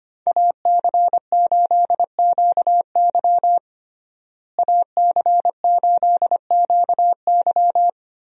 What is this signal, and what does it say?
Morse code, heard as a keyed tone, about 25 wpm, 700 Hz. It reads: AC8QY AC8QY
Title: USA Call Signs - 5 Characters - Repeated 2x - 25wpm